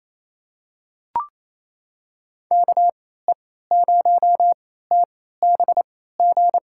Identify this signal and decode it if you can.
Morse code, heard as a keyed tone, about 28 wpm, spaced as for 16 wpm, 700 Hz. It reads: KE0TBG
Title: USA Call Signs - 6 Characters - 3x Character Spacing - 28wpm